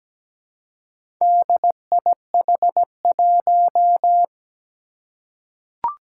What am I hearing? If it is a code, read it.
Morse code, heard as a keyed tone, about 17 wpm, 700 Hz. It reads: DIH1
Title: Four Challenging Letter-Number Combinations 17wpm